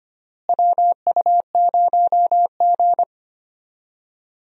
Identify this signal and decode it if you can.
Morse code, heard as a keyed tone, about 25 wpm, 700 Hz. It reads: WU0G
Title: USA Call Signs - 4 Characters - Repeated 2x - 25wpm